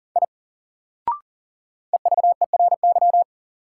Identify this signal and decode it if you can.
Morse code, heard as a keyed tone, about 40 wpm, 700 Hz. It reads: I EVERY
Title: Sentences from Top 200 Words - New Word Drill - 40wpm